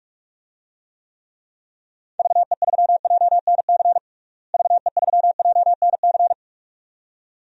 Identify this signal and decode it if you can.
Morse code, heard as a keyed tone, about 41 wpm, 700 Hz. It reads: VE3JNC VE3JNC
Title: DXCC-100 Call Signs - 6 Characters - Repeated 2x - 45wpm